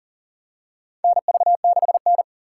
Morse code, heard as a keyed tone, about 40 wpm, 700 Hz. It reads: NV6N
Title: USA Call Signs - 3x Character Spacing - 40wpm